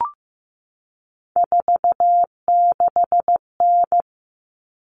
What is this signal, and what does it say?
Morse code, heard as a keyed tone, about 15 wpm, 700 Hz. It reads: 46N
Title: All Cut Numbers - 3-Digit 15wpm